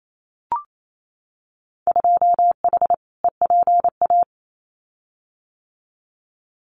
Morse code, heard as a keyed tone, about 28 wpm, 700 Hz. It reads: 2H EPA
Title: Winter Field Day Exchanges Encore 28wpm